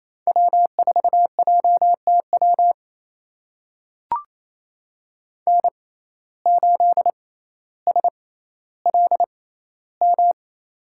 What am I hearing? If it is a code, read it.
Morse code, heard as a keyed tone, about 28 wpm, spaced as for 14 wpm, 700 Hz. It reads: W4JTW N8SLM
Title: USA Call Signs - 5 Characters - 6x Character Spacing - 28wpm